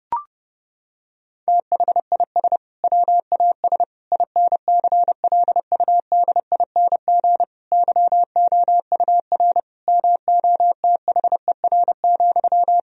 Morse code, heard as a keyed tone, about 32 wpm, 700 Hz. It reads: THIS WAS INCLUDING YOUR MOTHER,
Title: Sentences from Top 1000 Words - Review Part 4 - 30wpm